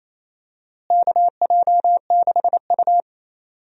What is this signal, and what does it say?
Morse code, heard as a keyed tone, about 28 wpm, 700 Hz. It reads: KJ6U
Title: USA Call Signs - 4 Characters - Repeated 3x - 28wpm